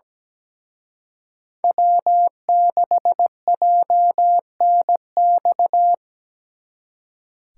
Morse code, heard as a keyed tone, about 20 wpm, 700 Hz. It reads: W6JNX
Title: Speed Racing - USA Call Signs 17wpm